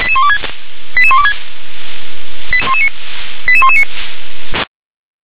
Five-tone selective call (ZVEI), heard as quick runs of five tones, tones of about 70 ms. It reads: 88126 88119